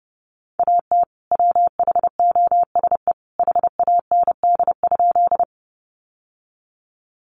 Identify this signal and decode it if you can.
Morse code, heard as a keyed tone, about 30 wpm, 700 Hz. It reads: AT WHOSE HAND?
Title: Sentences from Top 1200 Words 30wpm